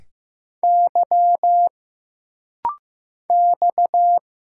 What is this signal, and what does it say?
Morse code, heard as a keyed tone, about 15 wpm, 700 Hz. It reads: Y X
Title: Single Letters - Rapid-Fire 15wpm